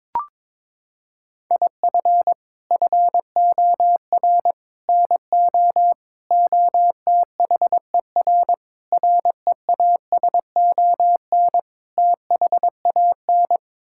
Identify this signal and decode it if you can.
Morse code, heard as a keyed tone, about 22 wpm, 700 Hz. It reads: IF FOR NO OTHER REASON THAN
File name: practice / 22wpm / Sets of 6 Words - 22wpm